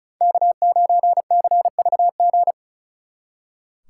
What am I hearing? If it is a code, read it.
Morse code, heard as a keyed tone, about 43 wpm, 700 Hz. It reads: K9CVG
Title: Speed Racing - USA Call Signs 35wpm